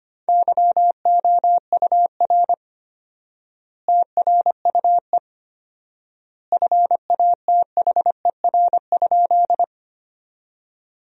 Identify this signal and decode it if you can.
Morse code, heard as a keyed tone, about 25 wpm, 700 Hz. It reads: YOUR TRUE FATHER?
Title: 4x Word-Spacing - Sentences from Top 400 Words - 25wpm